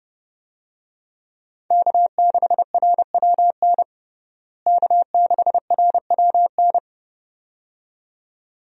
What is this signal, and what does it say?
Morse code, heard as a keyed tone, about 32 wpm, 700 Hz. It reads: K6RWN K6RWN
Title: USA Call Signs - 5 Characters - Repeated 2x - 30wpm